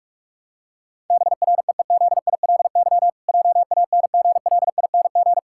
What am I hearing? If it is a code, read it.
Morse code, heard as a keyed tone, about 43 wpm, 700 Hz. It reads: BREEZILY JANGLING
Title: Pangram Challenge v2 - 45wpm